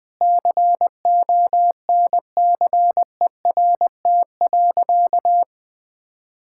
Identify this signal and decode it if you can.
Morse code, heard as a keyed tone, about 20 wpm, 700 Hz. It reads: CONCERT.
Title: Sentences with 8-Word Phrases - 20wpm